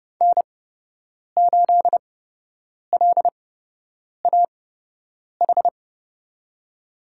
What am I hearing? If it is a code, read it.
Morse code, heard as a keyed tone, about 30 wpm, spaced as for 8 wpm, 700 Hz. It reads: N8LAH
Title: USA Call Signs - 5 Characters - 8x Character Spacing - 30wpm